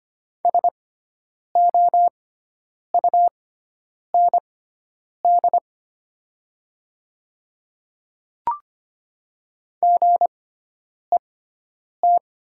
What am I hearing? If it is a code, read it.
Morse code, heard as a keyed tone, about 25 wpm, spaced as for 9 wpm, 700 Hz. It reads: SOUND GET
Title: 6x Character Spacing - Top 200 Words - 25wpm